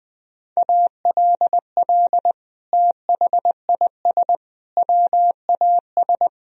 Morse code, heard as a keyed tone, about 21 wpm, 700 Hz. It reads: ALL THIS WAS
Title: Sentences from Top 1000 Words - Review Part 2 - 20wpm